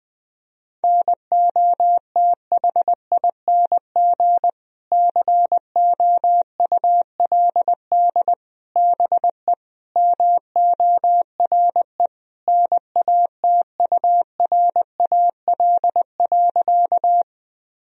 Morse code, heard as a keyed tone, about 20 wpm, 700 Hz. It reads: NOTHING COULD BE MORE NATURAL.
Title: Sentences from Top 1000 Words - Review Part 4 - 20wpm